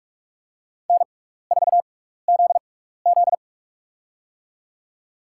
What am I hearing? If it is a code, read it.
Morse code, heard as a keyed tone, about 44 wpm, spaced as for 15 wpm, 700 Hz. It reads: N4ZZ
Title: USA Call Signs - 6x Character Spacing - 45wpm